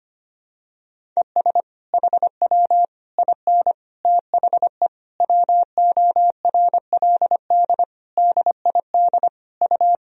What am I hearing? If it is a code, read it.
Morse code, heard as a keyed tone, about 25 wpm, 700 Hz. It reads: ES HW IN THE WORLD DID U
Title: Sentences from Top 200 Words and Abbreviations 25wpm